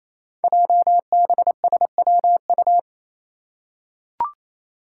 Morse code, heard as a keyed tone, about 29 wpm, 700 Hz. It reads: JBSWU
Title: Code Groups - 5 Letter:Numbers - 28wpm